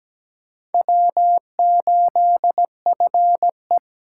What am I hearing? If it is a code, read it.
Morse code, heard as a keyed tone, about 20 wpm, 700 Hz. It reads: W8FE
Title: Speed Racing - USA Call Signs 17wpm